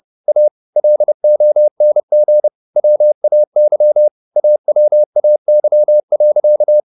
Morse code, heard as a keyed tone, about 30 wpm, 600 Hz. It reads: A LONG WAY AWAY.
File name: the old man and the sea / the old man and the sea 1 30wpm